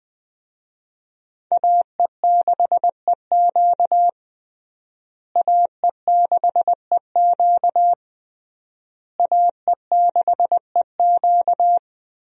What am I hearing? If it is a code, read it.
Morse code, heard as a keyed tone, about 21 wpm, 700 Hz. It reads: AE6EQ AE6EQ AE6EQ
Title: USA Call Signs - 5 Characters - Repeated 3x - 20wpm